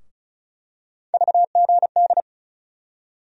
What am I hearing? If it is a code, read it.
Morse code, heard as a keyed tone, about 36 wpm, 700 Hz. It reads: VGD
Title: USA Call Sign Postfixes - 3 Letters 35wpm